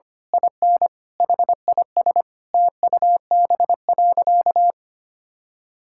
Morse code, heard as a keyed tone, about 25 wpm, 700 Hz. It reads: IN HIS TUB.
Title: Pangram Challenge v2 - 25wpm